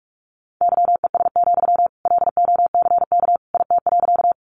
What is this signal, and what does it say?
Morse code, heard as a keyed tone, about 44 wpm, 700 Hz. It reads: YES, LOCK IT.
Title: Sentences from Top 1300 Words 45wpm